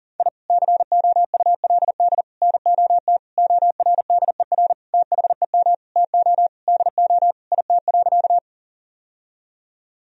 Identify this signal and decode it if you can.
Morse code, heard as a keyed tone, about 40 wpm, 700 Hz. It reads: I COULD NOT ORDER THEM TO DO IT.
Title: Sentences from Top 1000 Words - Review Part 2 - 40wpm